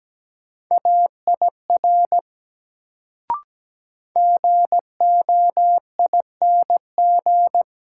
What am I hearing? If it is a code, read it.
Morse code, heard as a keyed tone, about 17 wpm, 700 Hz. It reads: AIR GOING
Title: Sets of 2 Words - New Word Drill - 17wpm